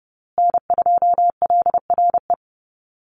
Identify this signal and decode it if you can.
Morse code, heard as a keyed tone, about 30 wpm, 700 Hz. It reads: N2LRE
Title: USA Call Signs Repeated 3x 30wpm